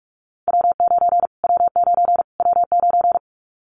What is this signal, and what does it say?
Morse code, heard as a keyed tone, about 44 wpm, 700 Hz. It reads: W9 W9 W9
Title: Callsign Prefixes for Top 25 DXCC - Repeated 3x - 45wpm